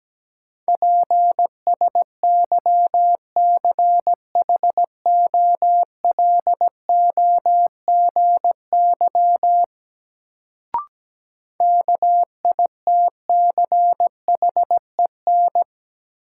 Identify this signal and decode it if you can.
Morse code, heard as a keyed tone, about 17 wpm, 700 Hz. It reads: PSYCHOLOGY KITCHEN
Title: Sentences from Top 1400 Words - New Word Drill - 17wpm